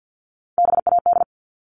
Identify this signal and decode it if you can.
Morse code, heard as a keyed tone, about 50 wpm, 700 Hz. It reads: BAD
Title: Top 501-750 Words 50wpm